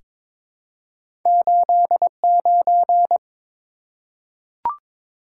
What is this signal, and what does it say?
Morse code, heard as a keyed tone, about 22 wpm, 700 Hz. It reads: 89
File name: practice / 22wpm / Two Letter-Number Combinations - 22wpm